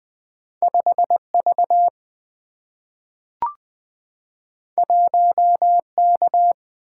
Morse code, heard as a keyed tone, about 20 wpm, 700 Hz. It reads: 5V 1K
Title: Single-Letter Single-Number 20wpm